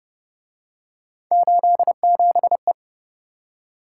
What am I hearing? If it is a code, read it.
Morse code, heard as a keyed tone, about 29 wpm, 700 Hz. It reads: 87E